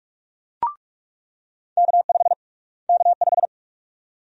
Morse code, heard as a keyed tone, about 42 wpm, 700 Hz. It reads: K5 K5
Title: USA Call Sign Prefixes - Repeated 2x - 45wpm